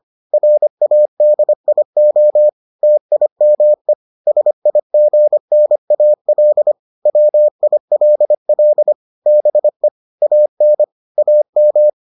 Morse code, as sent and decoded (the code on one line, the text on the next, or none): .-. .- -.. .. --- / - .. -- . / ... .. --. -. .- .-.. / .-- .. .-.. .-.. / -... / .- -. / .- --
RADIO TIME SIGNAL WILL BE AN AM